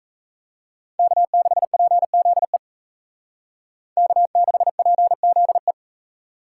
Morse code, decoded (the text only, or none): K6PZE K6PZE